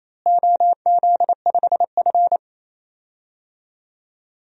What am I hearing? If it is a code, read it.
Morse code, heard as a keyed tone, about 28 wpm, 700 Hz. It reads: OZ5F
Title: DXCC-100 Call Signs - 4 Characters - Repeated 3x - 28wpm